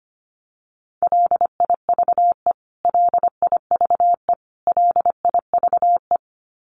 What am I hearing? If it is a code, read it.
Morse code, heard as a keyed tone, about 25 wpm, 700 Hz. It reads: LIVE LIVE LIVE
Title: Top 300 Words Repeated 25wpm